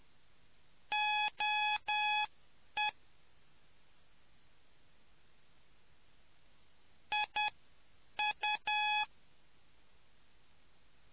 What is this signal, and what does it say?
Morse code, heard as a keyed tone, about 10 wpm, spaced as for 4 wpm, 840 Hz. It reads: OEIU